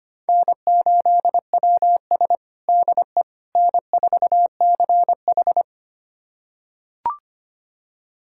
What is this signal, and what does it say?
Morse code, heard as a keyed tone, about 25 wpm, 700 Hz. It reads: N8WS DE N4CH
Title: USA Callsign DE Callsign - 4 Characters - 25wpm